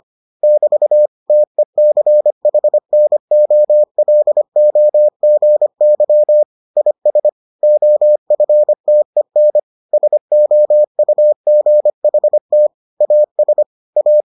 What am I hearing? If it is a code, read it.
Morse code, heard as a keyed tone, about 25 wpm, 600 Hz. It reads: <BT> TECHNOLOGY IS OFTEN SOUGHT AS A